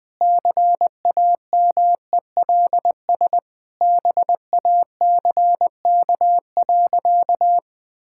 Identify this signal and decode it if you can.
Morse code, heard as a keyed tone, about 20 wpm, 700 Hz. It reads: CAMELS BACK.
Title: Sentences with 7-Word Phrases - 20wpm